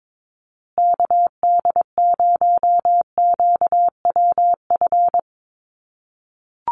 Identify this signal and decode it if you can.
Morse code, heard as a keyed tone, about 22 wpm, 700 Hz. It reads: KD0QWF
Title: USA Call Signs Repeated 3x 22wpm